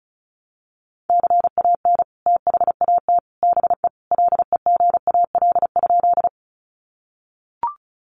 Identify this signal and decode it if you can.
Morse code, heard as a keyed tone, about 34 wpm, 700 Hz. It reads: CAN THAT BE LEGAL?
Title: Sentences from Top 1200 Words 35wpm